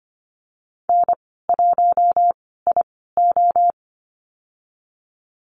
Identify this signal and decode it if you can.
Morse code, heard as a keyed tone, about 25 wpm, spaced as for 16 wpm, 700 Hz. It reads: N1IO
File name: Lesson 15g - Number 5 - Call Signs 25wpm